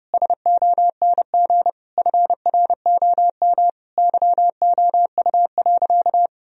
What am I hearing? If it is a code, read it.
Morse code, heard as a keyed tone, about 32 wpm, 700 Hz. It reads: SONG FROM YOU.
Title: Sentences from Top 1000 Words - Review Part 2 - 30wpm